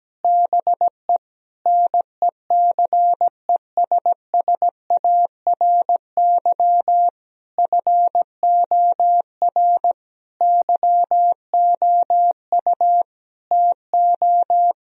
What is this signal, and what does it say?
Morse code, heard as a keyed tone, about 17 wpm, 700 Hz. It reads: BE NECESSARY FOR YOU TO
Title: Sentences from Top 1000 Words - Review Part 1 - 17wpm